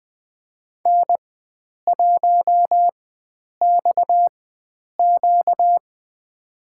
Morse code, heard as a keyed tone, about 20 wpm, spaced as for 9 wpm, 700 Hz. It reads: N1XQ
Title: USA Call Signs - 4 Characters - 4x Character Spacing - 20wpm